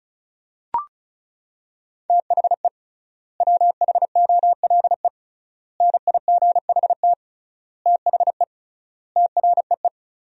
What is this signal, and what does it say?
Morse code, heard as a keyed tone, about 34 wpm, 700 Hz. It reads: THE WHOLE NIGHT THE TREE
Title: 3x Word-Spacing - Sentences from Top 500 Words - 35wpm